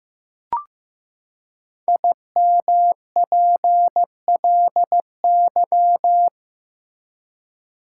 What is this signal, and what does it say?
Morse code, heard as a keyed tone, about 16 wpm, 700 Hz. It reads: IMPLY
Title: Five Letter Words - 15wpm